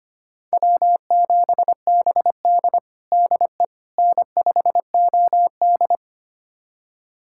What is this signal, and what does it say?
Morse code, heard as a keyed tone, about 25 wpm, 700 Hz. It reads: W7BD DE N5OD
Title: USA Callsign DE Callsign - 4 Characters - 25wpm